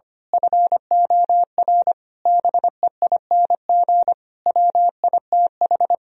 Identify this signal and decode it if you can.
Morse code, heard as a keyed tone, about 24 wpm, 700 Hz. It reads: FOR BEING WITH
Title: Speed-Racing - Sets of 5 Words 25wpm